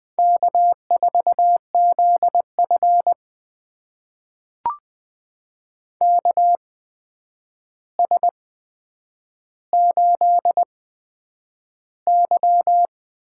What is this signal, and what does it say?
Morse code, heard as a keyed tone, about 20 wpm, spaced as for 9 wpm, 700 Hz. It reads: K4ZF KS8Y
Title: USA Call Signs - 4 Characters - 8x Character Spacing - 20wpm